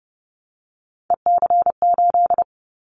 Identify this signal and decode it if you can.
Morse code, heard as a keyed tone, about 30 wpm, 700 Hz. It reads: EC8
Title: Callsign Prefixes for Top 50 DXCC - 30wpm